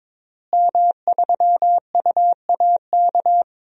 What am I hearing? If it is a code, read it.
Morse code, heard as a keyed tone, about 22 wpm, 700 Hz. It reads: M3UAK
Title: DXCC-100 Call Signs - 5 Characters - Repeated 2x - 22wpm